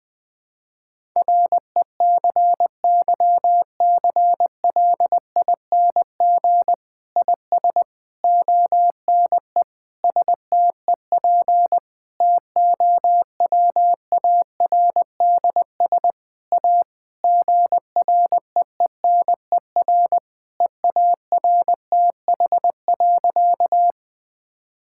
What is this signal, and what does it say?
Morse code, heard as a keyed tone, about 20 wpm, 700 Hz. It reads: RECYCLING IS ONE STEP TOWARDS A GREENER EARTH.
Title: Difficult Sentences with Hints - 20wpm